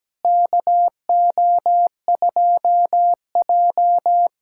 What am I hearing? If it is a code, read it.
Morse code, heard as a keyed tone, about 17 wpm, 700 Hz. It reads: KO2J
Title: USA Callsign DE Callsign - 4 Characters - 17wpm